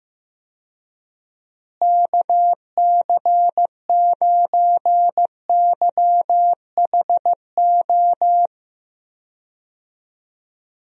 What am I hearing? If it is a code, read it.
Morse code, heard as a keyed tone, about 15 wpm, 700 Hz. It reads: KC9YHO
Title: USA Call Signs - 6 Characters - Repeated 2x - 15wpm